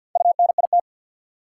Morse code, heard as a keyed tone, about 52 wpm, 700 Hz. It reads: UNIT
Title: Four Letter Words - 50wpm